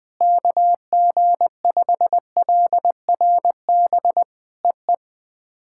Morse code, heard as a keyed tone, about 20 wpm, 700 Hz. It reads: KG5LRB EE
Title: POTA QSO Practice - 20wpm